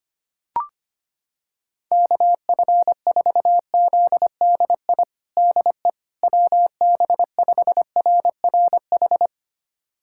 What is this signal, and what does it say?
Morse code, heard as a keyed tone, about 25 wpm, 700 Hz. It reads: KF4ZDI DE WB5RRH
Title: USA Callsign DE Callsign - 6 Characters - 25wpm